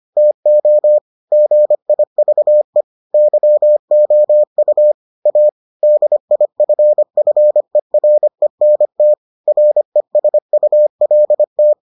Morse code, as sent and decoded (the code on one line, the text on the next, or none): - --- / --. .. ...- . / -.-- --- ..- / .- / -.. .. ..-. ..-. .-. -. - / .-. ... ..- .-.. -
TO GIVE YOU A DIFFERENT RESULT